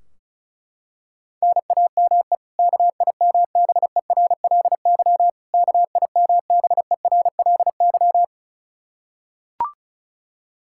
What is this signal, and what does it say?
Morse code, heard as a keyed tone, about 34 wpm, 700 Hz. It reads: NAME KIMBERLY KIMBERLY
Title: Practice QSO - Exchange 1 - Beginning - 35wpm